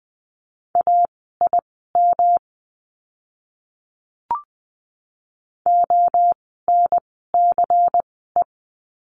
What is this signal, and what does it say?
Morse code, heard as a keyed tone, about 20 wpm, spaced as for 15 wpm, 700 Hz. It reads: AIM ONCE
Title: 108 Lesson 18f - Letter m - Words 20wpm